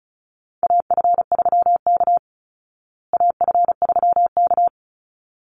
Morse code, heard as a keyed tone, about 34 wpm, 700 Hz. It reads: AF3K AF3K
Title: Call Signs Repeated 3x - 35wpm